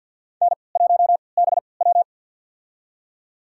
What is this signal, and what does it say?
Morse code, heard as a keyed tone, about 53 wpm, spaced as for 29 wpm, 700 Hz. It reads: N1BW
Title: USA Call Signs - 4 Characters - 3x Character Spacing - 50wpm